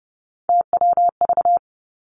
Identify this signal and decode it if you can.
Morse code, heard as a keyed tone, about 30 wpm, 700 Hz. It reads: TWV
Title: Three Challenging Letter-Number Combinations 30wpm